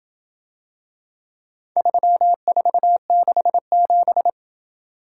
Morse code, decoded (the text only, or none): 3467